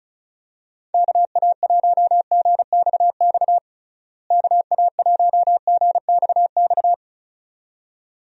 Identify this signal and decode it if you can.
Morse code, heard as a keyed tone, about 36 wpm, 700 Hz. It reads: KA1GXX KA1GXX
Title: USA Call Signs - 6 Characters - Repeated 2x - 35wpm